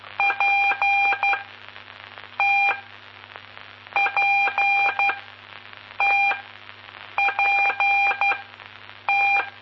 Morse code, heard as a keyed tone, about 12 wpm, spaced as for 6 wpm, 2400 Hz, with radio noise. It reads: PTPTPT
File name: pp5bar / aulas de telegrafia / Aula-11